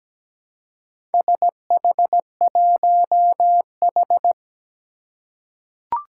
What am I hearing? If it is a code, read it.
Morse code, heard as a keyed tone, about 17 wpm, 700 Hz. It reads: SH1H
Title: Four Challenging Letter-Number Combinations 17wpm